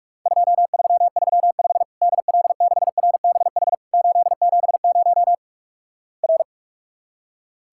Morse code, heard as a keyed tone, about 44 wpm, 700 Hz. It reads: 2335 DL6RBH 870 R
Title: Worked All Europe DX Contest - QTC Exchange - 45wpm